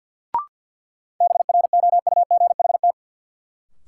Morse code, heard as a keyed tone, about 52 wpm, 700 Hz. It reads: BROUGHT
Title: Sentences from Top 500 Words - New Word Drill - 50wpm